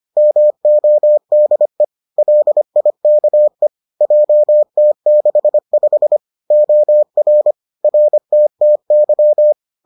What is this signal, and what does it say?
Morse code, heard as a keyed tone, about 25 wpm, 600 Hz. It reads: MODE LIKE JT65 OR RTTY